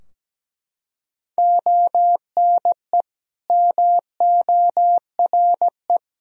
Morse code, heard as a keyed tone, about 17 wpm, 700 Hz. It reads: ONE MORE
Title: Sentences from Top 1000 Words - Review Part 2 - 17wpm